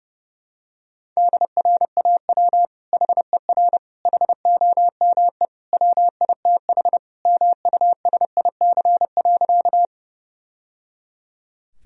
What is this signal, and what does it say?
Morse code, heard as a keyed tone, about 32 wpm, 700 Hz. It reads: DRAW HER HOME WITH MUSIC.
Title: Sentences from Top 1000 Words - Review Part 1 - 30wpm